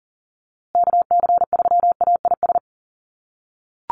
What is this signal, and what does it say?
Morse code, heard as a keyed tone, about 40 wpm, 700 Hz. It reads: KC3AIS
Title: USA Call Signs Repeated 3x 40wpm